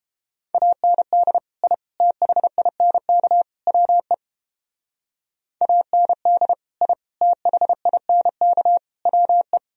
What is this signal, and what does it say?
Morse code, heard as a keyed tone, about 31 wpm, 700 Hz. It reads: AND I THINK WE AND I THINK WE